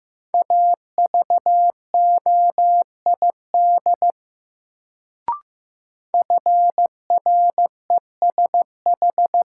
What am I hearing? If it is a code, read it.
Morse code, heard as a keyed tone, about 15 wpm, 700 Hz. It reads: AVOID FRESH
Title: Sentences from Top 1200 Words - New Word Drill Alt - 15wpm